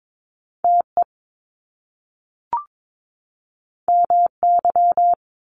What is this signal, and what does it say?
Morse code, heard as a keyed tone, about 22 wpm, 700 Hz. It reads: TE MY
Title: Top Two Letter Combinations in Words 22wpm